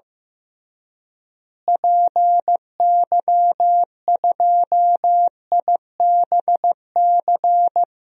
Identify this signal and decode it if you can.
Morse code, heard as a keyed tone, about 15 wpm, 700 Hz. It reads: PY2IBC
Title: DXCC-100 Call Signs - 6 Characters - Repeated 3x - 15wpm